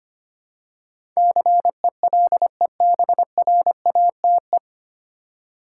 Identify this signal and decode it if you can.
Morse code, heard as a keyed tone, about 25 wpm, 700 Hz. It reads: CELEBRATE
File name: Nine Letter Words - 25wpm